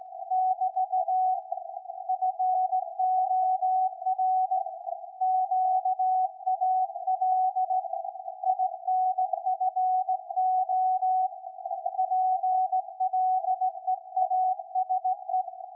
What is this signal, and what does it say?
Morse code, heard as a keyed tone, about 17 wpm, 750 Hz. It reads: <BT> FOR QAL INFO PLEASE
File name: iz7slz / VLF / SAQ 13-02-2023